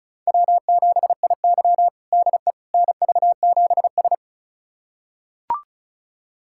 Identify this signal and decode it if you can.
Morse code, heard as a keyed tone, about 36 wpm, 700 Hz. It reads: W7IY DE NV7S
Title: USA Callsign DE Callsign - 4 Characters - 35wpm